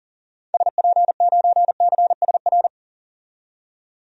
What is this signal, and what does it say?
Morse code, heard as a keyed tone, about 40 wpm, 700 Hz. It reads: SP9CSR
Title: DXCC-100 Call Signs - 6 Characters - 40wpm